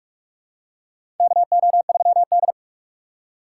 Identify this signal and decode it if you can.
Morse code, heard as a keyed tone, about 44 wpm, 700 Hz. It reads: KO3D
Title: USA Call Signs - 4 Characters - 6x Character Spacing - 45wpm